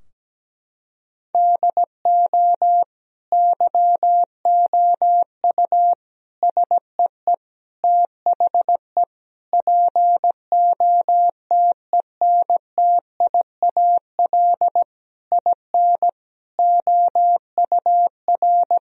Morse code, heard as a keyed tone, about 17 wpm, 700 Hz. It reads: DO YOU SEE THE POTENTIAL IN OUR
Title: Sentences from Top 1000 Words - Review Part 4 - 17wpm